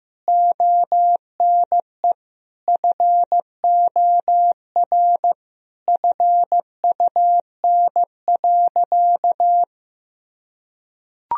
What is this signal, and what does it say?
Morse code, heard as a keyed tone, about 15 wpm, 700 Hz. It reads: ONE FOR FUN.
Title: Sentences from Top 1000 Words - Review Part 3 - 15wpm